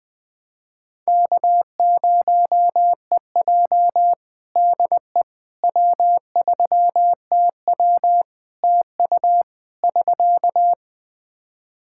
Morse code, heard as a keyed tone, about 20 wpm, 700 Hz. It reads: K0EJ DE W3TW TU <SK>
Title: Practice QSO - Exchange 4 - Ending - 20wpm